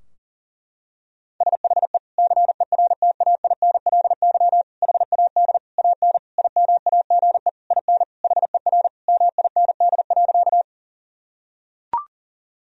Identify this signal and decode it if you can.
Morse code, heard as a keyed tone, about 40 wpm, 700 Hz. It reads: SHE CERTAINLY HAD AN IMAGE IN HER MIND.
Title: Sentences from Top 1000 Words - Review Part 4 - 40wpm